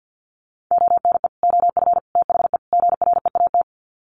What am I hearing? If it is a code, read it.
Morse code, heard as a keyed tone, about 50 wpm, 700 Hz. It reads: ONE OF THE GREAT
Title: Sets of 4 Words 50wpm